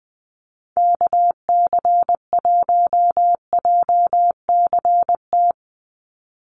Morse code, heard as a keyed tone, about 20 wpm, 700 Hz. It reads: KC1JCT
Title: USA Call Signs 20wpm